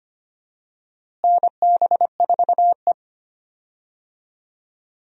Morse code, heard as a keyed tone, about 25 wpm, 700 Hz. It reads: NB4E
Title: USA Call Signs - 4 Characters - Repeated 2x - 25wpm